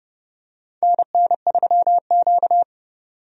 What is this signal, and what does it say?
Morse code, heard as a keyed tone, about 32 wpm, 700 Hz. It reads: NN3Q
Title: USA Call Signs - 4 Characters - 30wpm